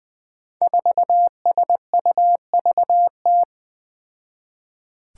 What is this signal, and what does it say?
Morse code, heard as a keyed tone, about 20 wpm, 700 Hz. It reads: 4SUVT
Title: Five Challenging Letter-Number Combinations 20wpm